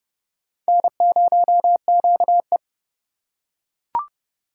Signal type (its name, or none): Morse code